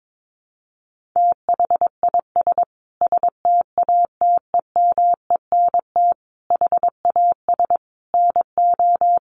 Morse code, heard as a keyed tone, about 22 wpm, 700 Hz. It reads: THIS STATEMENT HAS NO
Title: Sentences from Top 1200 Words 22wpm